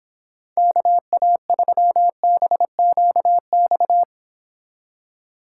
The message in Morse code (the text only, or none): KA3BQX